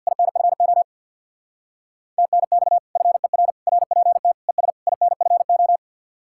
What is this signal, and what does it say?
Morse code, heard as a keyed tone, about 61 wpm, 700 Hz. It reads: INFO TNX FER RPT ES INFO
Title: Speed-Racing Common QSO Phrases - 50wpm